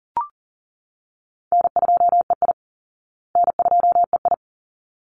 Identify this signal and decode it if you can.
Morse code, heard as a keyed tone, about 40 wpm, 700 Hz. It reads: N2EI N2EI
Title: Call Signs Repeated 3x - 40wpm